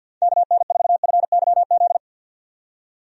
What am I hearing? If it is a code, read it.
Morse code, heard as a keyed tone, about 50 wpm, 700 Hz. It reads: KN4FYZ
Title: USA Call Signs - 6 Characters - 50wpm